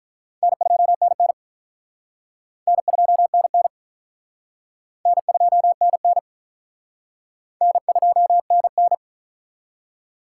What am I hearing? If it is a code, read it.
Morse code, heard as a keyed tone, about 45 wpm, 700 Hz. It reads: N2NN N2NN N2NN N2NN